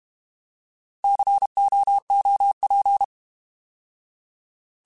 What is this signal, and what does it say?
Morse code, heard as a keyed tone, about 32 wpm, 800 Hz. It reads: COOP
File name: Alphabetical / Alphabetical mp3s / Cs @30